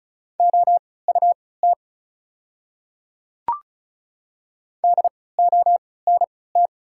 Morse code, heard as a keyed tone, about 36 wpm, spaced as for 20 wpm, 700 Hz. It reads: OUT DONT
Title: 3x Character Spacing - Top 300 Words - 35wpm